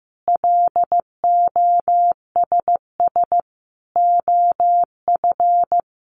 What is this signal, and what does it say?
Morse code, heard as a keyed tone, about 14 wpm, 700 Hz. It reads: LOSS OF
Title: Sentences from Top 1400 Words 15wpm